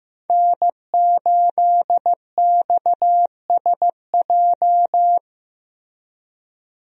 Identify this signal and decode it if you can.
Morse code, heard as a keyed tone, about 16 wpm, 700 Hz. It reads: N8XSJ
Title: USA Call Signs - 5 Characters - Repeated 3x - 15wpm